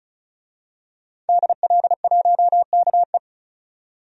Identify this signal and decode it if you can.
Morse code, heard as a keyed tone, about 36 wpm, 700 Hz. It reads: DL1KE
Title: DXCC-100 Call Signs - 5 Characters - 35wpm